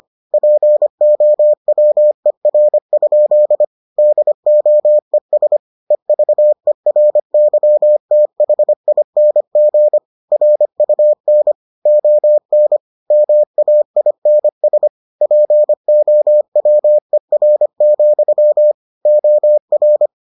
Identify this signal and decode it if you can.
Morse code, heard as a keyed tone, about 25 wpm, 600 Hz. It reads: POWER? DOES EVERYTHING RUN ON MAINS POWER, OR